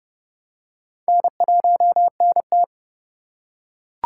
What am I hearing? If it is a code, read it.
Morse code, heard as a keyed tone, about 29 wpm, 700 Hz. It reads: N1NT